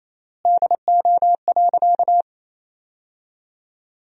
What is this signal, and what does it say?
Morse code, heard as a keyed tone, about 28 wpm, 700 Hz. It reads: DO.
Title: 3x Word-Spacing - Sentences from Top 100 Words - 28wpm